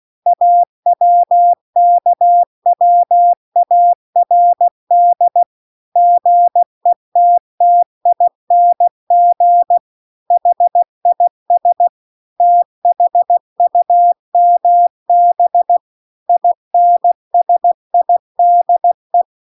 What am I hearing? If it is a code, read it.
Morse code, heard as a keyed tone, about 16 wpm, 700 Hz. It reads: AWKWARD GETTING HIS THUMB INSIDE